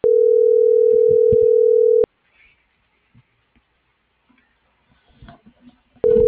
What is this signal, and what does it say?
Ringback tone (North America), 440 + 480 Hz.
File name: CENTRAL MAERSK / out nogueronmsk